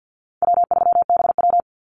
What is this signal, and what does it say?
Morse code, heard as a keyed tone, about 50 wpm, 700 Hz. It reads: W3BW
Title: Easy Call Signs - 50wpm